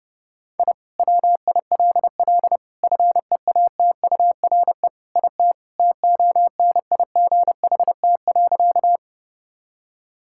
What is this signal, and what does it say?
Morse code, heard as a keyed tone, about 32 wpm, 700 Hz. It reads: I WILL FEATURE IT TONIGHT.
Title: Sentences from Top 1000 Words - Review Part 5 - 30wpm